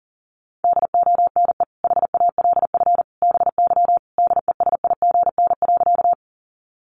Morse code, heard as a keyed tone, about 40 wpm, 700 Hz. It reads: DONE HALF BY DESIGN.
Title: Sentences from Top 800 Words 40wpm